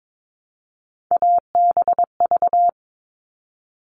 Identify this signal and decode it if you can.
Morse code, heard as a keyed tone, about 22 wpm, 700 Hz. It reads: ABV
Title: All Cut Numbers - 3-Digit 22wpm